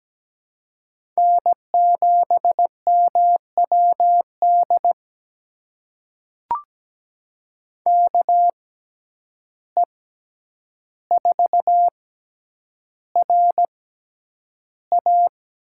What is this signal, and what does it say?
Morse code, heard as a keyed tone, about 17 wpm, spaced as for 9 wpm, 700 Hz. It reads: N7MWD KE4RA
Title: USA Call Signs - 5 Characters - 6x Character Spacing - 17wpm